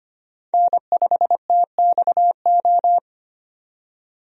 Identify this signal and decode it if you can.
Morse code, heard as a keyed tone, about 25 wpm, 700 Hz. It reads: N5TXO
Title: USA Call Signs - 5 Characters - 25wpm